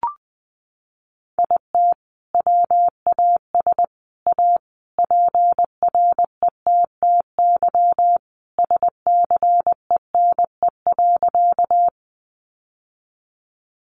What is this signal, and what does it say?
Morse code, heard as a keyed tone, about 20 wpm, 700 Hz. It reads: IT WAS A PRETTY SCENE.
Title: Sentences from Top 1100 Words 20wpm